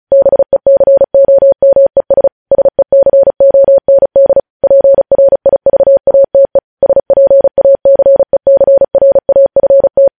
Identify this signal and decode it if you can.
Morse code, heard as a keyed tone, about 36 wpm, 550 Hz. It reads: BECOMES SECOND PRIVATE SPACECRAFT
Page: News Headlines in Morse Code at 35 WPM